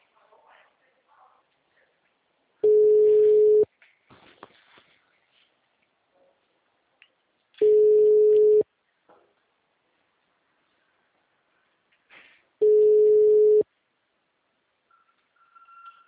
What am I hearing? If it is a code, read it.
Ringback tone (Europe, 425 Hz).